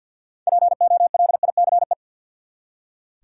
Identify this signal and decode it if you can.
Morse code, heard as a keyed tone, about 45 wpm, 700 Hz. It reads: POLICE
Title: Six Letter Words - 50wpm